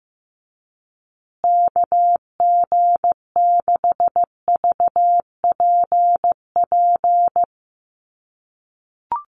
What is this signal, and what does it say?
Morse code, heard as a keyed tone, about 15 wpm, 700 Hz. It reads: KG6VPP
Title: USA Call Signs Repeated 3x 15wpm